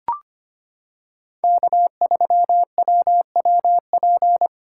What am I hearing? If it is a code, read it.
Morse code, heard as a keyed tone, about 25 wpm, 700 Hz. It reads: K3WWP
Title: K1USN SST QSO Practice - 25wpm